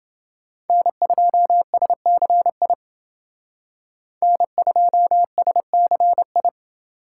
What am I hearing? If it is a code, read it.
Morse code, heard as a keyed tone, about 29 wpm, 700 Hz. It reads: N2SCI N2SCI